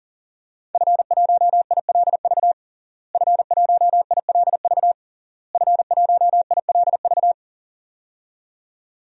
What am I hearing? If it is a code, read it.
Morse code, heard as a keyed tone, about 40 wpm, 700 Hz. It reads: F1ILV F1ILV F1ILV
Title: DXCC-100 Call Signs - 5 Characters - Repeated 3x - 40wpm